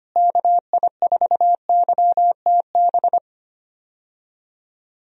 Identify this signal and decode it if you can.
Morse code, heard as a keyed tone, about 25 wpm, 700 Hz. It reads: KI4YTB
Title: USA Call Signs - 6 Characters - Repeated 2x - 25wpm